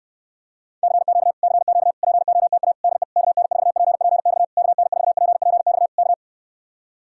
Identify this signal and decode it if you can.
Morse code, heard as a keyed tone, about 64 wpm, 700 Hz. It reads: CQ CQ POTA DE KN4PPX KN4PPX K